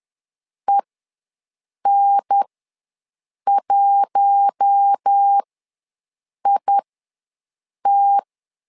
Morse code, heard as a keyed tone, about 10 wpm, spaced as for 6 wpm, 800 Hz. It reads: EN1IT
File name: LEZIONI / 4 Q1 ws 10wpm 2538